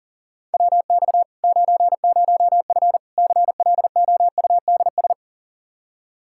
Morse code, heard as a keyed tone, about 40 wpm, 700 Hz. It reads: WX 90F CLOUDS
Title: Practice QSO - Exchange 2 - Info - 40wpm